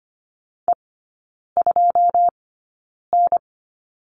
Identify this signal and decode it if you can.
Morse code, heard as a keyed tone, about 25 wpm, spaced as for 9 wpm, 690 Hz. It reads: E2N
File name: Lesson 14e - Number 2 - Three Characters 25wpm